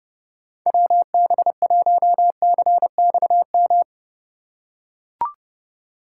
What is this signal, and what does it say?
Morse code, heard as a keyed tone, about 30 wpm, 700 Hz. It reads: WB1CXM